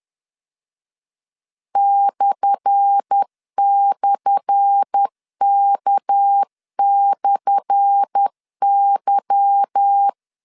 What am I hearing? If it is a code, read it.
Morse code, heard as a keyed tone, about 10 wpm, 800 Hz. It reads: //K/Y
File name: LEZIONI / 10 SIGNS / 10A SLASH 10wpm 2808